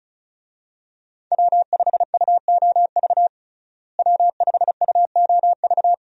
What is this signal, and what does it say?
Morse code, heard as a keyed tone, about 36 wpm, 700 Hz. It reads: W5UOV W5UOV
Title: USA Call Signs - 5 Characters - Repeated 2x - 35wpm